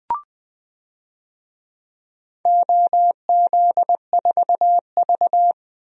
Morse code, heard as a keyed tone, about 20 wpm, 700 Hz. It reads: OZ4V
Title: DXCC-100 Call Signs - 4 Characters - 20wpm